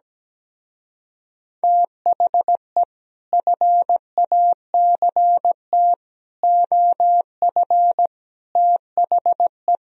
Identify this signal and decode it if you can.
Morse code, heard as a keyed tone, about 17 wpm, 700 Hz. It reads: THE FACT OF THE
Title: Speed-Racing - Sets of 5 Words 17wpm